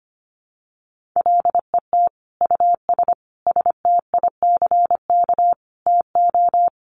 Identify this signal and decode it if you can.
Morse code, heard as a keyed tone, about 25 wpm, 700 Hz. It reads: LET US STICK TO
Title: Sentences from Top 1200 Words 25wpm